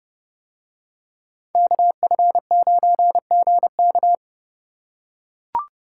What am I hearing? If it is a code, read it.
Morse code, heard as a keyed tone, about 29 wpm, 700 Hz. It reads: KF9GK